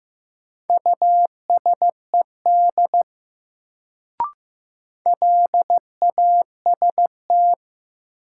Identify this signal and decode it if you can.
Morse code, heard as a keyed tone, about 15 wpm, 700 Hz. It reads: USED LAST
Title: Sets of 2 Words - New Word Drill - 15wpm